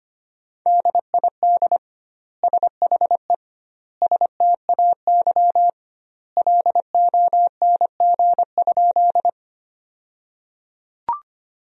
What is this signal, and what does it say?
Morse code, heard as a keyed tone, about 25 wpm, 700 Hz. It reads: DID SHE STAY LONG?
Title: 2x Word-Spacing - Sentences from Top 500 Words - 25wpm